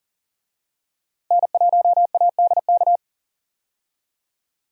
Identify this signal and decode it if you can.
Morse code, heard as a keyed tone, about 40 wpm, 700 Hz. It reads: N1ADK
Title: USA Call Signs - 5 Characters - Repeated 3x - 40wpm